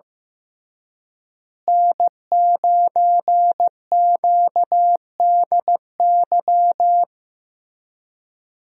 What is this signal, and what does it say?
Morse code, heard as a keyed tone, about 15 wpm, 700 Hz. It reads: N9QDY
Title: USA Call Signs - 5 Characters - Repeated 3x - 15wpm